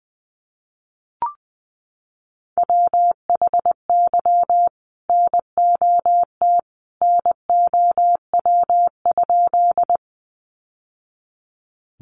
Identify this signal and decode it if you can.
Morse code, heard as a keyed tone, about 20 wpm, 700 Hz. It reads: WHY NOT NOW?
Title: Sentences from Top 600 Words 20wpm